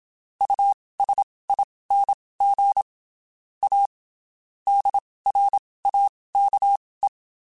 Morse code, heard as a keyed tone, about 27 wpm, spaced as for 19 wpm, 800 Hz. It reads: USING A DRAKE
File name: (spaced) / QSOs Spaced-mp3s / WD0FBT 1 @25 sp